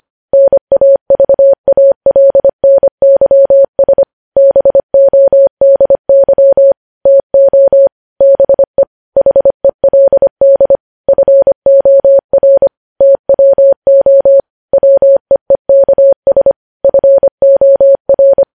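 Morse code, heard as a keyed tone, about 25 wpm, 550 Hz. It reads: NAVALNYS BODY TO BE HELD FOR TWO WEEKS FOR
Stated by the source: Morse code news headlines sent at 25 wpm.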